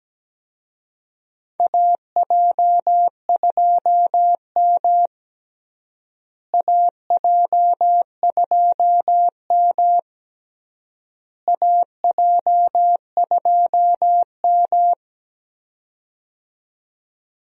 Morse code, heard as a keyed tone, about 17 wpm, 700 Hz. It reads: AJ2M AJ2M AJ2M
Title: USA Call Signs - 4 Characters - Repeated 3x - 17wpm